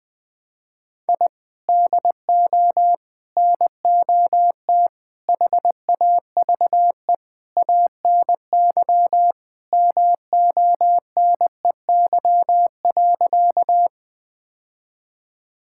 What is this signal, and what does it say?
Morse code, heard as a keyed tone, about 20 wpm, 700 Hz. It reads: I DO NOT HAVE ANY MONEY.
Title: Sentences with 5-Word Phrases - 20wpm